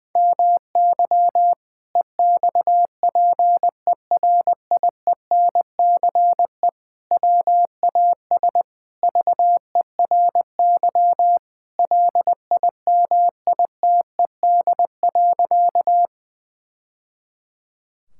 Morse code, heard as a keyed tone, about 20 wpm, 700 Hz. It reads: MY EXPERIENCE WAS VERY LIMITED.
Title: Sentences from Top 1000 Words - Review Part 6 - 20wpm